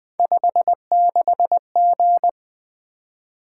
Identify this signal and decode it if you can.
Morse code, heard as a keyed tone, about 20 wpm, 700 Hz. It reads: 56G
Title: Three Letter-Number Combinations - 20wpm